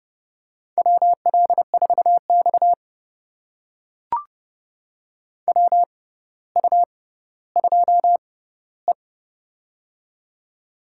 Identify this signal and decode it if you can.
Morse code, heard as a keyed tone, about 30 wpm, spaced as for 15 wpm, 700 Hz. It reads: WL4X WU2E
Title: USA Call Signs - 4 Characters - 6x Character Spacing - 30wpm